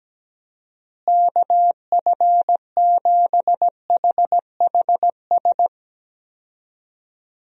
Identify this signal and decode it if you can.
Morse code, heard as a keyed tone, about 17 wpm, 700 Hz. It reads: KF7HHS
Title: USA Call Signs - 6 Characters - Repeated 2x - 17wpm